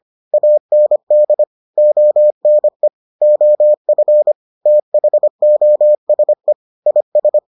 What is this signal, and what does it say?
Morse code, heard as a keyed tone, about 25 wpm, 600 Hz. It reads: AND ONE OF THOSE IS